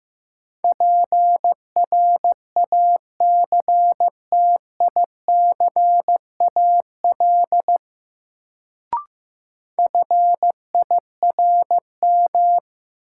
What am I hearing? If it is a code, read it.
Morse code, heard as a keyed tone, about 15 wpm, 700 Hz. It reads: PRACTICAL FIRM
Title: Sentences from Top 1400 Words - New Word Drill Alt - 15wpm